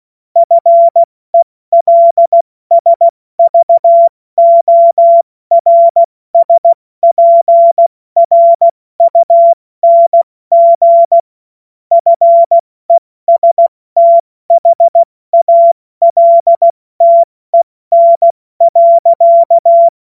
Morse code, heard as a keyed tone, about 16 wpm, 680 Hz. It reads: FELSVORSPRUNG FESTHALTEN.
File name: Morsekurs / Witze / Witz 11. Januar 2022, 70 F80